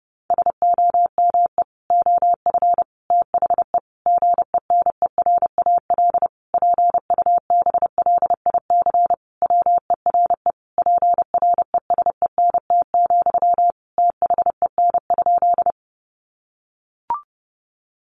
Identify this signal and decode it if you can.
Morse code, heard as a keyed tone, about 30 wpm, 700 Hz. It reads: SOME OF THE GENERAL PUBLIC WERE PRESENT, THEN?
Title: Sentences from Top 800 Words 30wpm